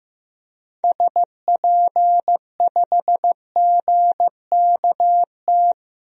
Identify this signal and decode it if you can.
Morse code, heard as a keyed tone, about 15 wpm, 700 Hz. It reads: SP5GKT
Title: DXCC-100 Call Signs - 6 Characters - Repeated 3x - 15wpm